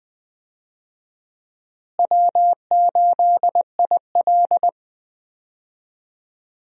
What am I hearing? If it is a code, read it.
Morse code, heard as a keyed tone, about 20 wpm, 700 Hz. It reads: W8IL
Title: USA Call Signs - 4 Characters - Repeated 2x - 20wpm